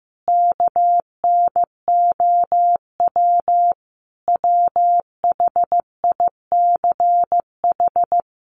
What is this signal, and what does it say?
Morse code, heard as a keyed tone, about 14 wpm, 700 Hz. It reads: KNOW WHICH
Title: Sentences from Top 1300 Words 15wpm